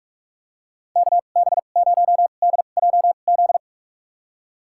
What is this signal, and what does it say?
Morse code, heard as a keyed tone, about 47 wpm, spaced as for 33 wpm, 700 Hz. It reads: KB0DJZ
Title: USA Call Signs - 2x Character Spacing - 45wpm